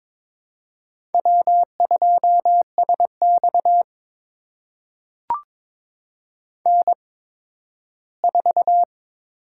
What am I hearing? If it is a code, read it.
Morse code, heard as a keyed tone, about 22 wpm, spaced as for 13 wpm, 700 Hz. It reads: W2SX N4
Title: USA Call Signs - 4 Characters - 8x Character Spacing - 22wpm